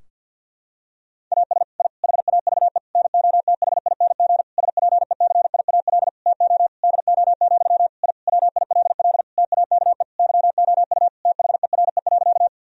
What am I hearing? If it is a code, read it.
Morse code, heard as a keyed tone, about 50 wpm, 700 Hz. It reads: AS I HAVE NOTHING SPECIAL TO DO, I WILL TAKE YOU THERE.
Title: Sentences from Top 1000 Words - Review Part 3 - 50wpm